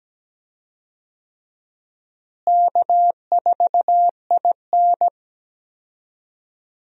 Morse code, heard as a keyed tone, about 17 wpm, 700 Hz. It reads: K4IN
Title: USA Call Signs - 4 Characters - Repeated 3x - 17wpm